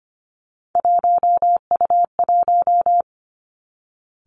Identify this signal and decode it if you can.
Morse code, heard as a keyed tone, about 25 wpm, 700 Hz. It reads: 1U1
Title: Three Challenging Letter-Number Combinations 25wpm